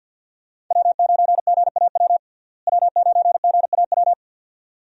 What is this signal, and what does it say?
Morse code, heard as a keyed tone, about 50 wpm, 700 Hz. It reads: W9GAW W9GAW
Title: USA Call Signs - 5 Characters - Repeated 2x - 50wpm